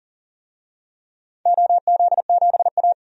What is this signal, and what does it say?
Morse code, heard as a keyed tone, about 40 wpm, 700 Hz. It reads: OZ7A
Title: DXCC-100 Call Signs - 4 Characters - 40wpm